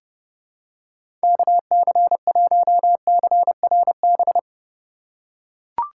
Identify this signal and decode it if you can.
Morse code, heard as a keyed tone, about 30 wpm, 700 Hz. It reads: KC1CRB
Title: USA Call Signs - 2x Character Spacing - 30wpm